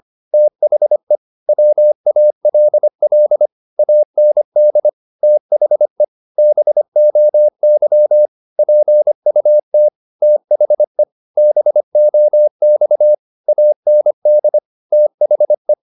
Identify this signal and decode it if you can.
Morse code, heard as a keyed tone, about 25 wpm, 600 Hz. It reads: THE WALL AND THE BOY PUT THE BOX AND THE
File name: the old man and the sea part 1 25wpm